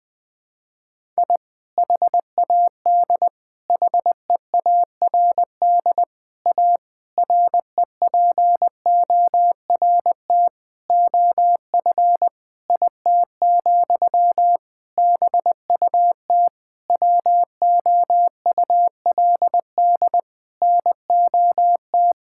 Morse code, heard as a keyed tone, about 21 wpm, 700 Hz. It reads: I HAD HEARD A REPORT OF IT, BUT WOULD NOT
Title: Sentences from Top 1000 Words - Review Part 6 - 20wpm